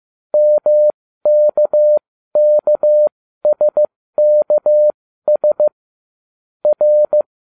Morse code, heard as a keyed tone, about 15 wpm, spaced as for 13 wpm, 600 Hz. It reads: MKKSKS R